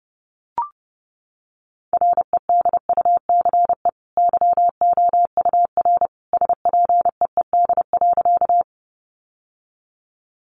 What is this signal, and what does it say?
Morse code, heard as a keyed tone, about 30 wpm, 700 Hz. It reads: REDUCE YOUR SPEED.
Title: Sentences from Top 1100 Words 30wpm